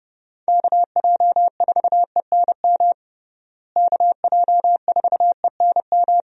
Morse code, heard as a keyed tone, about 30 wpm, 700 Hz. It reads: KJ4ENM KJ4ENM
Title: USA Call Signs - 6 Characters - Repeated 2x - 30wpm